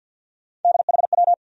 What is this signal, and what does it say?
Morse code, heard as a keyed tone, about 47 wpm, 700 Hz. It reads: DHW
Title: USA Call Sign Postfixes - 3 Letters 50wpm